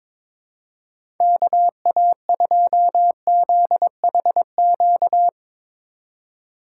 Morse code, heard as a keyed tone, about 22 wpm, 700 Hz. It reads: KA2ZHQ
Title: USA Call Signs - 4x Character Spacing - 22wpm